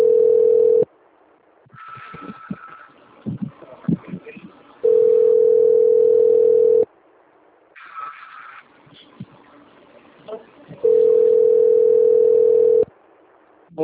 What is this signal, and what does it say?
Ringback tone (North America), 440 + 480 Hz, with speech.
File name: rg-1600-unknown- - - .45298